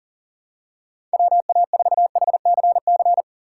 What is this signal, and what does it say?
Morse code, heard as a keyed tone, about 40 wpm, 700 Hz. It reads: WA4HCC
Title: K1USN SST QSO Practice - 40wpm